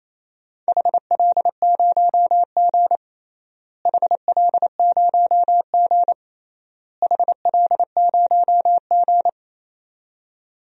Morse code, heard as a keyed tone, about 28 wpm, 700 Hz. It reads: HL0G HL0G HL0G
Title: DXCC-100 Call Signs - 4 Characters - Repeated 3x - 28wpm